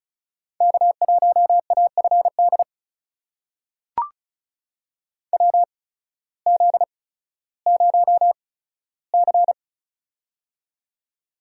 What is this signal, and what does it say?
Morse code, heard as a keyed tone, about 36 wpm, spaced as for 16 wpm, 700 Hz. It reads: K1AFD WZ0C
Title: USA Call Signs - 8x Character Spacing - 35wpm